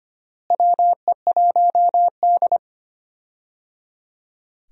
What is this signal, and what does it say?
Morse code, heard as a keyed tone, about 25 wpm, 700 Hz. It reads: WE1D
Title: USA Call Signs - 4 Characters - Repeated 3x - 25wpm